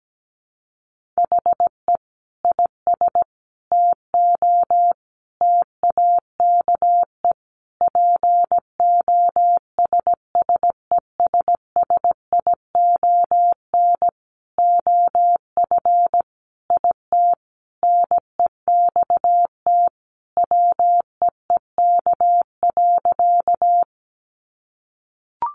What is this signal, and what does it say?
Morse code, heard as a keyed tone, about 17 wpm, 700 Hz. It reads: HE IS TO TAKE POSSESSION OF IT NEXT WEEK.
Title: Sentences from Top 1400 Words 17wpm